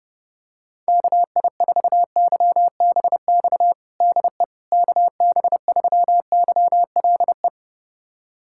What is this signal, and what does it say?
Morse code, heard as a keyed tone, about 31 wpm, 700 Hz. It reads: KI4YBX DE KB3YLE
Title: USA Callsign DE Callsign - 6 Characters - 30wpm